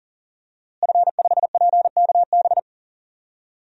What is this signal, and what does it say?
Morse code, heard as a keyed tone, about 40 wpm, 700 Hz. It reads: F5PKB
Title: DXCC-100 Call Signs - 5 Characters - Repeated 3x - 40wpm